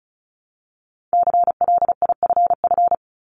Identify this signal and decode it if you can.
Morse code, heard as a keyed tone, about 36 wpm, 700 Hz. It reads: CLIFF
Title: Multi-letter Phonemes 35wpm